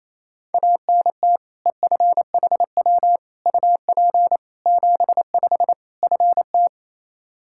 Morse code, heard as a keyed tone, about 27 wpm, 700 Hz. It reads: ANT EFHW UP 75 FT
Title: K1USN SST QSO Practice 28wpm